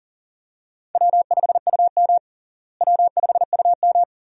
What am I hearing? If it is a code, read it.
Morse code, heard as a keyed tone, about 37 wpm, 700 Hz. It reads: W5UM W5UM
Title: USA Call Signs - 4 Characters - Repeated 2x - 40wpm